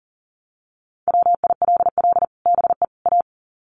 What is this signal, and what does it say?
Morse code, heard as a keyed tone, about 40 wpm, 700 Hz. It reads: WILL BE A
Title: Sets of 3 Words 40wpm